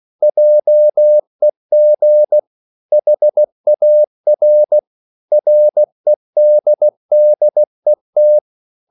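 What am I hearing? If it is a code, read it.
Morse code, heard as a keyed tone, about 16 wpm, 600 Hz. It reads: JEG HAR REDDET
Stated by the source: Slow (QRS) Morse code at 16 wpm, from CW med Gnister.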